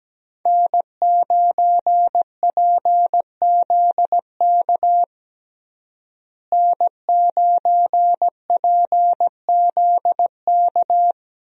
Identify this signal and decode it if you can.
Morse code, heard as a keyed tone, about 17 wpm, 700 Hz. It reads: N9PZK N9PZK
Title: USA Call Signs - 5 Characters - Repeated 3x - 17wpm